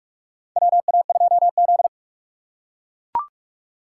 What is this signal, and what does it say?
Morse code, heard as a keyed tone, about 41 wpm, 700 Hz. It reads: WA2Z
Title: USA Call Signs - 4 Characters - 4x Character Spacing - 45wpm